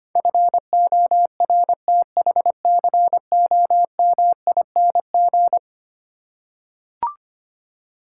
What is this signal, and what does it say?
Morse code, heard as a keyed tone, about 25 wpm, 700 Hz. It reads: FORTHCOMING
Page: morse-code-ninja-practice-25wpm